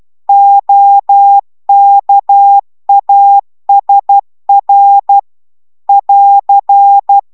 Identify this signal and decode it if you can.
Morse code, heard as a keyed tone, about 12 wpm, 800 Hz. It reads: OKASR <AR>